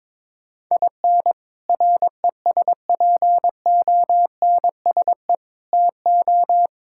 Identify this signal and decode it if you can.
Morse code, heard as a keyed tone, about 22 wpm, 700 Hz. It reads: IN RESPONSE TO
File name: practice / 22wpm / Sets of 6 Words - 22wpm